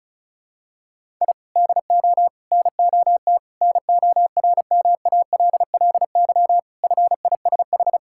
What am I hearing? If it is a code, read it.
Morse code, heard as a keyed tone, about 36 wpm, 700 Hz. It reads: I DO NOT NORMALLY FISH
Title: Sentences from Top 1000 Words - Review Part 6 - 35wpm